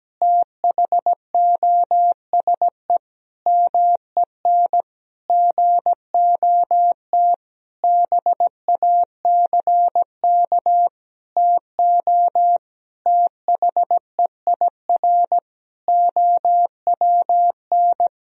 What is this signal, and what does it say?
Morse code, heard as a keyed tone, about 17 wpm, 700 Hz. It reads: THOSE MEN GOT BACK TO THEIR OWN
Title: Sentences from Top 1000 Words - Review Part 1 - 17wpm